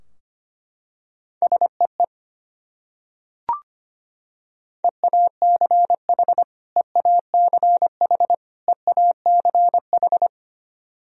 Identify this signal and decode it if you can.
Morse code, heard as a keyed tone, about 25 wpm, 700 Hz. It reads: SEE EACH EACH EACH
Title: Top 100 Words Repeated 25wpm